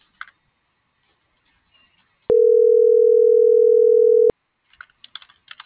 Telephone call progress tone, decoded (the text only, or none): ringback tone